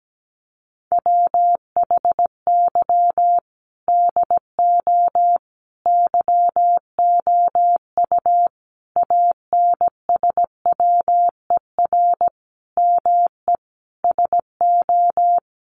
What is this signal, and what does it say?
Morse code, heard as a keyed tone, about 17 wpm, 700 Hz. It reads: WHY DO YOU ANSWER ME SO
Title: Sentences from Top 1500 Words 17wpm